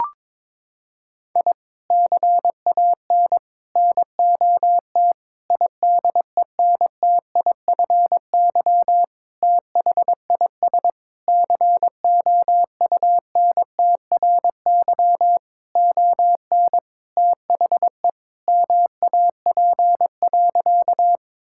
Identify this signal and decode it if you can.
Morse code, heard as a keyed tone, about 22 wpm, 700 Hz. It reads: I CAN NOT IDENTIFY THIS COUNTRY ON THE MAP.
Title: Sentences from Top 1000 Words - Review Part 6 - 22wpm